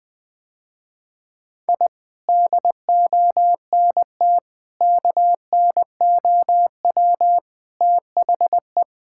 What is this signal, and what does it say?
Morse code, heard as a keyed tone, about 20 wpm, 700 Hz. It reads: I DONT KNOW THE
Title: Sentences with 7-Word Phrases - Spoken Beginning - 20wpm